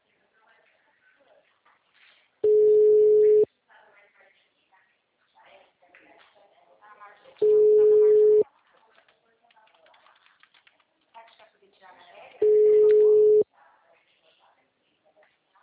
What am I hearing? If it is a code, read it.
Ringback tone (Europe, 425 Hz), with speech.
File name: out- -2002- - - .294